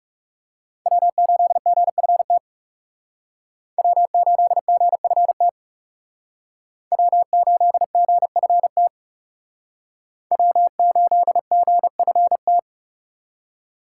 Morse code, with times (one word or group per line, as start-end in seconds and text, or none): 0.9-2.4 W8GFT
3.8-5.5 W8GFT
6.9-8.9 W8GFT
10.3-12.6 W8GFT